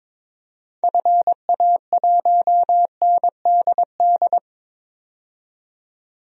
Morse code, heard as a keyed tone, about 22 wpm, 700 Hz. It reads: FA1NDD
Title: DXCC-100 Call Signs - 6 Characters - 22wpm